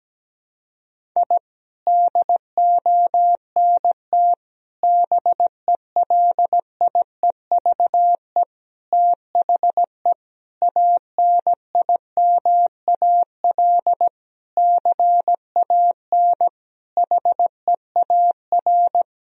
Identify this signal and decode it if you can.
Morse code, heard as a keyed tone, about 17 wpm, 700 Hz. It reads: I DONT BELIEVE THE ANIMAL CAN HEAR
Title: Sentences from Top 1000 Words - Review Part 1 - 17wpm